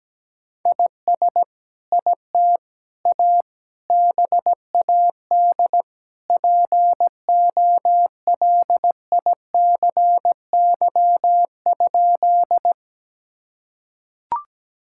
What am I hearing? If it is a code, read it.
Morse code, heard as a keyed tone, about 17 wpm, 700 Hz. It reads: IS IT A BAD POLICY?
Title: Sentences from Top 1000 Words - Review Part 5 - 17wpm